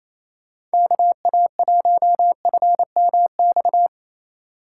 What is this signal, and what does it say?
Morse code, heard as a keyed tone, about 35 wpm, 700 Hz. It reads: KA1FMX